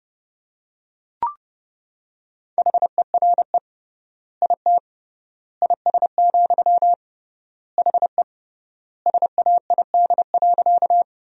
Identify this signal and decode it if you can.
Morse code, heard as a keyed tone, about 30 wpm, 700 Hz. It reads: HERE IT IS, HE SAID.
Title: 3x Word-Spacing - Sentences from Top 200 Words - 30wpm